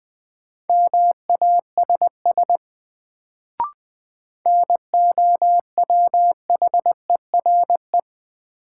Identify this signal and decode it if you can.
Morse code, heard as a keyed tone, about 20 wpm, 700 Hz. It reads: MASS NOWHERE
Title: Sets of 5 Words Encore - New Word Drill - 20wpm